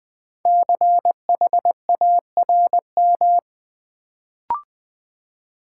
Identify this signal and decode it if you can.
Morse code, heard as a keyed tone, about 20 wpm, 700 Hz. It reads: CHARM
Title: Build A Word - 1 Letter At A Time - 5-Letter Words - 20wpm